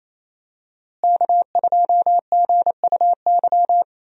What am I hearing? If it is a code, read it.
Morse code, heard as a keyed tone, about 28 wpm, 700 Hz. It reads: K2GUY
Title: USA Call Signs - 5 Characters - Repeated 3x - 28wpm